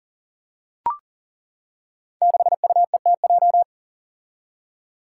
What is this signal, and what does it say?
Morse code, heard as a keyed tone, about 39 wpm, 700 Hz. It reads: 6UETJ
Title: Five Challenging Letter-Number Combinations 40wpm